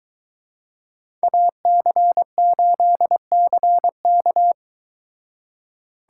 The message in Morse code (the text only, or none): AC8CK